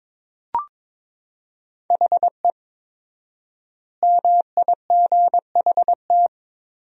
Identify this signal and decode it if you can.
Morse code, heard as a keyed tone, about 22 wpm, 700 Hz. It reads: HE MIGHT
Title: 4x Word-Spacing - Sentences from Top 300 Words - 22wpm